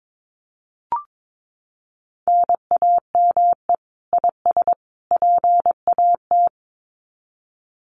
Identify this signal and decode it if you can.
Morse code, heard as a keyed tone, about 22 wpm, 700 Hz. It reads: NAME IS PAT
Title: Common QSO Phrases 22wpm